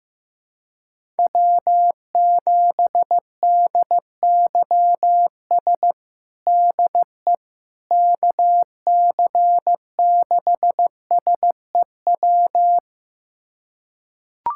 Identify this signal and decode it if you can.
Morse code, heard as a keyed tone, about 15 wpm, 700 Hz. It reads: W7DYS DE KC6SEW
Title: USA Callsign DE Callsign 15wpm